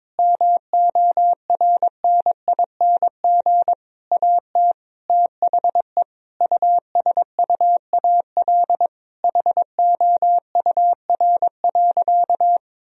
Morse code, heard as a keyed tone, about 22 wpm, 700 Hz. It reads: MORNING AT THE USUAL HOUR.
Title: Sentences from Top 1000 Words - Review Part 2 - 22wpm